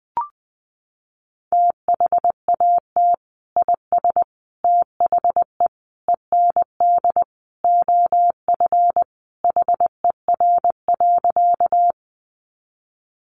Morse code, heard as a keyed tone, about 20 wpm, 700 Hz. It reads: THAT IS THE END OF HER.
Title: Sentences from Top 200 Words 20wpm